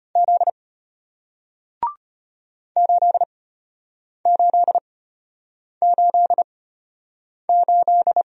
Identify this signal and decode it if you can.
Morse code, heard as a keyed tone, about 34 wpm, spaced as for 8 wpm, 690 Hz. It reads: Z 8888